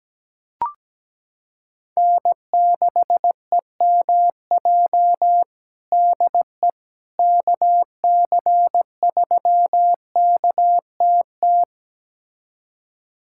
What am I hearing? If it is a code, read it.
Morse code, heard as a keyed tone, about 17 wpm, 700 Hz. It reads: N6EMJ DE KC3KTT
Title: USA Callsign DE Callsign 17wpm